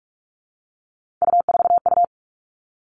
Morse code, heard as a keyed tone, about 46 wpm, 700 Hz. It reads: U4U
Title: Three Challenging Letter-Number Combinations 45wpm